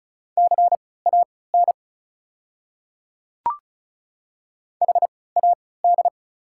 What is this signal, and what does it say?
Morse code, heard as a keyed tone, about 36 wpm, spaced as for 20 wpm, 690 Hz. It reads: CAN HAD
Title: 3x Character Spacing - Top 100 Words - 35wpm